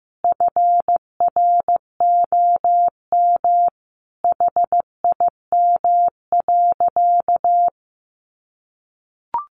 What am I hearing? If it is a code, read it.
Morse code, heard as a keyed tone, about 15 wpm, 700 Hz. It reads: FROM HIM.
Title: Sentences from Top 1500 Words 15wpm